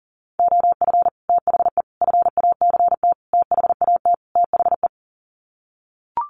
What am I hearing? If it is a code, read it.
Morse code, heard as a keyed tone, about 40 wpm, 700 Hz. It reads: OF THE FACT THAT THE
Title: Sets of 5 Words Encore2 40wpm